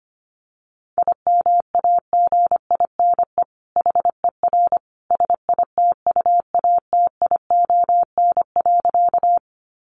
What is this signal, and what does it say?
Morse code, heard as a keyed tone, about 25 wpm, 700 Hz. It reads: IMAGINE HER SITUATION.
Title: Sentences from Top 1400 Words 25wpm